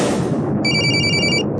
There is no code.